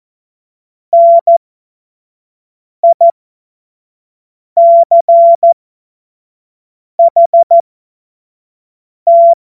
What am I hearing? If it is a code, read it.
Morse code, heard as a keyed tone, about 14 wpm, spaced as for 5 wpm, 680 Hz. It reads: NICHT
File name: Morsekurs / Witze / Witz 02. August 2022 25 F70